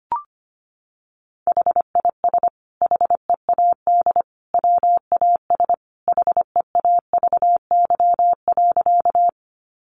Morse code, heard as a keyed tone, about 25 wpm, 700 Hz. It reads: HIS HEAD WAS HEAVY.
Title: Sentences from Top 1100 Words 25wpm